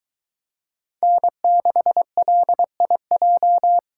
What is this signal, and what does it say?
Morse code, heard as a keyed tone, about 22 wpm, 700 Hz. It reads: N6LIJ